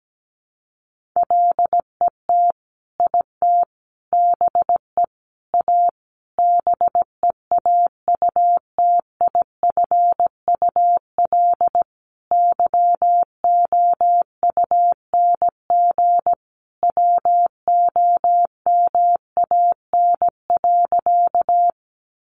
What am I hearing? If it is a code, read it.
Morse code, heard as a keyed tone, about 17 wpm, 700 Hz. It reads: LET IT BE A BEAUTIFUL YOUNG WOMAN.
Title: Sentences from Top 1100 Words 17wpm